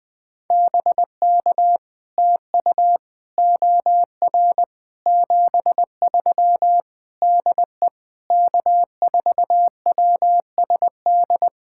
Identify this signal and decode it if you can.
Morse code, heard as a keyed tone, about 20 wpm, 700 Hz. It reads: BK TU OR 73 DE K4WSD
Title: POTA QSO Practice - 20wpm